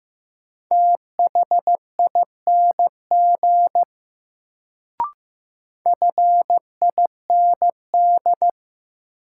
Morse code, heard as a keyed tone, about 15 wpm, 700 Hz. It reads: THING FIND
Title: Sentences from Top 100 Words - New Word Drill - 15wpm